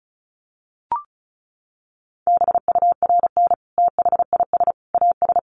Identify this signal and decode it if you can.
Morse code, heard as a keyed tone, about 34 wpm, 700 Hz. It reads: BURN THIS AS
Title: Sentences from Top 1300 Words 35wpm